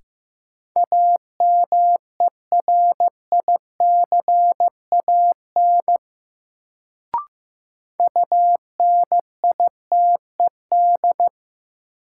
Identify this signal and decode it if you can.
Morse code, heard as a keyed tone, about 15 wpm, 700 Hz. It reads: AMERICAN UNITED
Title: Sets of 2 Words - New Word Drill - 15wpm